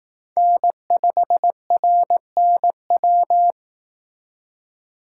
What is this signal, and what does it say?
Morse code, heard as a keyed tone, about 18 wpm, 700 Hz. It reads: N5RNW
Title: Speed Racing - USA Call Signs 15wpm